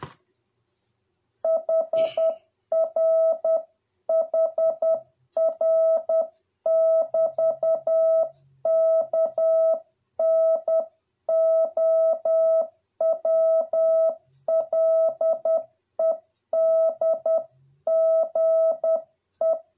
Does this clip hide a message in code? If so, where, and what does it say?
Yes: Morse code from 1.5 to 19.6 s, heard as a keyed tone, about 10 wpm, 650 Hz. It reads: HR HR <BT> KNOWLEDGE